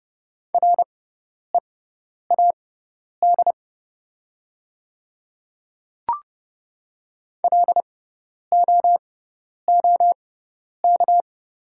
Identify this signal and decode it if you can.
Morse code, heard as a keyed tone, about 30 wpm, spaced as for 10 wpm, 700 Hz. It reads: READ LOOK
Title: 6x Character Spacing - Top 200 Words - 30wpm